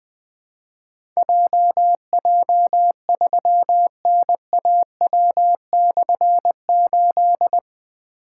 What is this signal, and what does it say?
Morse code, heard as a keyed tone, about 20 wpm, 700 Hz. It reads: JJ3NAW/8
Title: Difficult Call Signs 20wpm